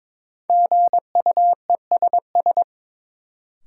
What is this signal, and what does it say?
Morse code, heard as a keyed tone, about 22 wpm, 700 Hz. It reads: GUESS
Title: Sentences from Top 1500 Words - New Word Drill - 22wpm